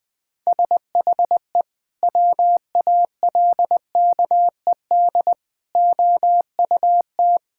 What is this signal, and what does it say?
Morse code, heard as a keyed tone, about 21 wpm, 700 Hz. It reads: SHE WALKED OUT
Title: Sentences with 3-Word Phrases - Spoken Beginning - 20wpm